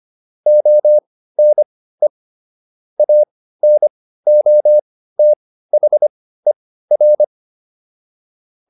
Morse code, heard as a keyed tone, about 25 wpm, spaced as for 15 wpm, 600 Hz. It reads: ONE ANOTHER